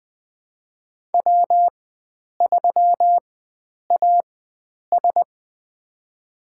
Morse code, heard as a keyed tone, about 20 wpm, spaced as for 9 wpm, 700 Hz. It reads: W3AS
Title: USA Call Signs - 4 Characters - 4x Character Spacing - 20wpm